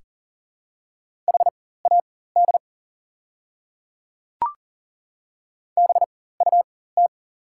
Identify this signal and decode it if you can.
Morse code, heard as a keyed tone, about 40 wpm, spaced as for 19 wpm, 690 Hz. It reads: HAD BUT
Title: 4x Character Spacing - Top 300 Words - 40wpm